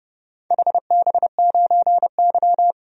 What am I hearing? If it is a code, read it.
Morse code, heard as a keyed tone, about 30 wpm, 700 Hz. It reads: HB9Y
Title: DXCC-100 Call Signs - 4 Characters - 30wpm